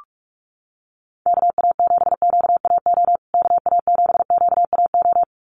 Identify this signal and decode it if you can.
Morse code, heard as a keyed tone, about 44 wpm, 700 Hz. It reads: KA7QAO KA7QAO
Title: USA Call Signs Repeated 2x 45wpm